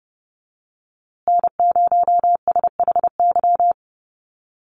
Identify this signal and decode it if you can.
Morse code, heard as a keyed tone, about 30 wpm, 700 Hz. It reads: N0SHY
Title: USA Call Signs Repeated 2x 30wpm